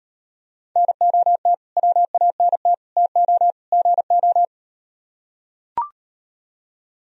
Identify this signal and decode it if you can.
Morse code, heard as a keyed tone, about 38 wpm, 700 Hz. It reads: NOT WANT TO GO